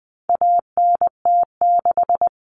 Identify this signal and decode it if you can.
Morse code, heard as a keyed tone, about 20 wpm, 700 Hz. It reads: ANT6
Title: Common Cut Numbers - 4-Digit 20wpm